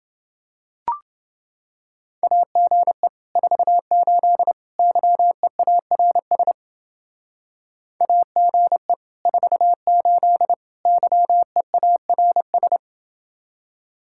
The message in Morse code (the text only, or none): AGE 48 YEARS AGE 48 YEARS